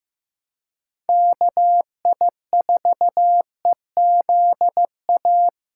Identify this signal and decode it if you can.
Morse code, heard as a keyed tone, about 16 wpm, 700 Hz. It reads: KI4EZA
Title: USA Callsign DE Callsign - 6 Characters - 15wpm